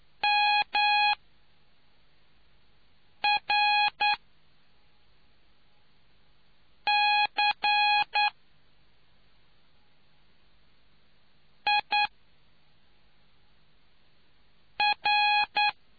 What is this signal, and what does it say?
Morse code, heard as a keyed tone, about 9 wpm, spaced as for 3 wpm, 830 Hz. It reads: MRCIR